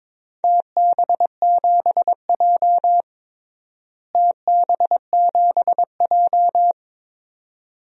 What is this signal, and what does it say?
Morse code, heard as a keyed tone, about 21 wpm, 700 Hz. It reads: TB7J TB7J
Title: DXCC-100 Call Signs - 4 Characters - Repeated 2x - 22wpm